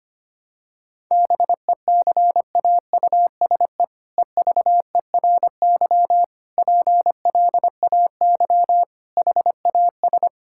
Morse code, heard as a keyed tone, about 25 wpm, 700 Hz. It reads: BECAUSE EVERY PLAY HAS
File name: practice / 25wpm / All-Age Appropriate Jokes - 25wpm